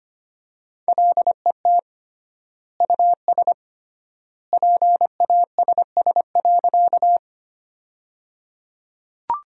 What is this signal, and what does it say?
Morse code, heard as a keyed tone, about 25 wpm, 700 Hz. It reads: LET US PASS.
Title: 3x Word-Spacing - Sentences from Top 400 Words - 25wpm